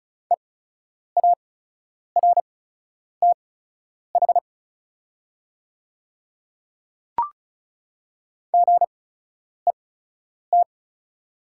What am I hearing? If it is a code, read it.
Morse code, heard as a keyed tone, about 36 wpm, spaced as for 10 wpm, 700 Hz. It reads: EARTH GET
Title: 8x Character Spacing - Top 200 Words - 35wpm